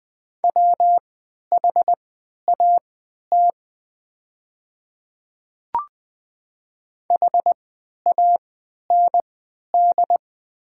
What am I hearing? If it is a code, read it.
Morse code, heard as a keyed tone, about 20 wpm, spaced as for 11 wpm, 700 Hz. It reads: WHAT HAND
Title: 3x Character Spacing - Top 200 Words - 20wpm